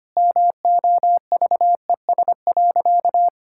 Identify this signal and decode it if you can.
Morse code, heard as a keyed tone, about 25 wpm, 700 Hz. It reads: MOVES.
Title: Pangram Challenge v2 - 25wpm